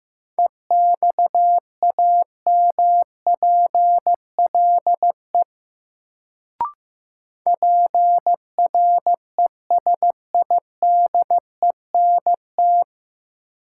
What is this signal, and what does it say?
Morse code, heard as a keyed tone, about 15 wpm, 700 Hz. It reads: EXAMPLE PRESIDENT
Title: Sets of 2 Words - New Word Drill - 15wpm